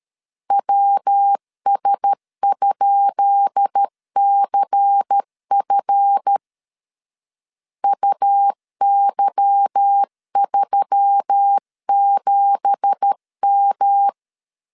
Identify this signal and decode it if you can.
Morse code, heard as a keyed tone, about 13 wpm, 800 Hz. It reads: WS?CF UY37M
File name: LEZIONI / 10 SIGNS / 10B INT 12wpm 2414